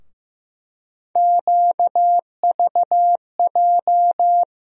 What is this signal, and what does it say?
Morse code, heard as a keyed tone, about 15 wpm, 700 Hz. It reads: QVJ
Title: USA Call Sign Postfixes - 3 Letters 15wpm